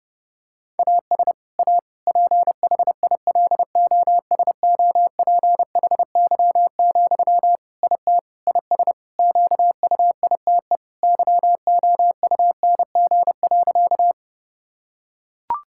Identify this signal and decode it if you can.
Morse code, heard as a keyed tone, about 30 wpm, 700 Hz. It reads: AS A PHILOSOPHY, IT IS QUITE YOUNG.
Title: Sentences from Top 1000 Words - Review Part 6 - 30wpm